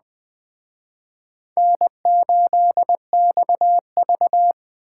Morse code, heard as a keyed tone, about 22 wpm, 700 Hz. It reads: N8XV